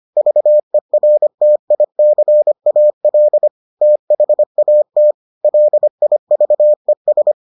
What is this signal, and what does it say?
Morse code, heard as a keyed tone, about 25 wpm, 600 Hz. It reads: VERTICAL THAT LIVES